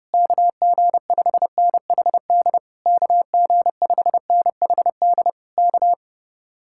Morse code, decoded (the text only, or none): KG5NHD KG5NHD K